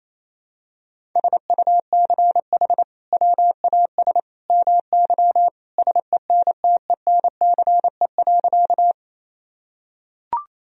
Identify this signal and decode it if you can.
Morse code, heard as a keyed tone, about 27 wpm, 700 Hz. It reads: SUCH WAS MY SENTENCE.
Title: Sentences from Top 1000 Words - Review Part 1 - 28wpm